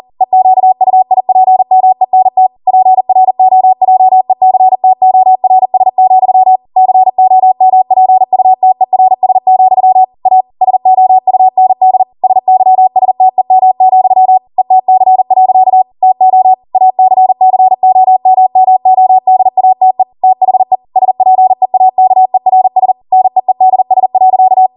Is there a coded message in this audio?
Yes: Morse code from 0.2 to 24.7 s, heard as a keyed tone, about 40 wpm, 750 Hz. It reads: EQUIPMENT PROJECTORS, COMPUTERS, A SOUND SYSTEM, ETC. TO ACCOMMODATE THE SPEAKERS NEEDS.